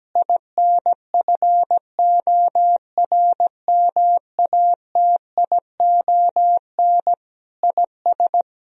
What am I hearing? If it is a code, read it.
Morse code, heard as a keyed tone, about 17 wpm, 700 Hz. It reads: INFORMATION IS
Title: Sentences from Top 1000 Words - Review Part 3 - 17wpm